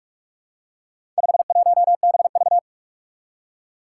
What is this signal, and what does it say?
Morse code, heard as a keyed tone, about 44 wpm, 700 Hz. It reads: 51BV
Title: Four Challenging Letter-Number Combinations 45wpm